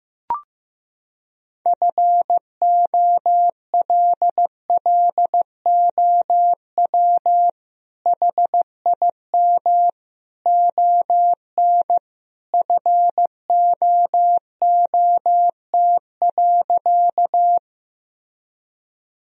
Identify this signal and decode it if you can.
Morse code, heard as a keyed tone, about 15 wpm, 700 Hz. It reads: FOLLOW HIM ON FOOT.
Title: Sentences from Top 1000 Words - Review Part 3 - 15wpm